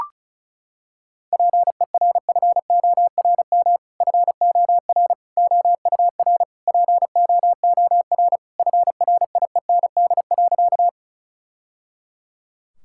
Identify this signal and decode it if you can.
Morse code, heard as a keyed tone, about 36 wpm, 700 Hz. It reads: PERFORM FOR OUR POOR FRIEND.
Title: Sentences from Top 1000 Words - Review Part 5 - 35wpm